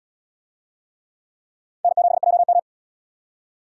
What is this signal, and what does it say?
Morse code, heard as a keyed tone, about 77 wpm, 700 Hz. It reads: N7JM